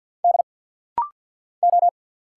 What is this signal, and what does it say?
Morse code, heard as a keyed tone, about 50 wpm, 700 Hz. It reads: D O
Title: Single Letters - Rapid-Fire 50wpm